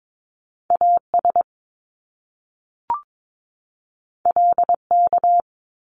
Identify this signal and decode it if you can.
Morse code, heard as a keyed tone, about 22 wpm, 700 Hz. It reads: AS LK
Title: Top Two Letter Combinations in Words 22wpm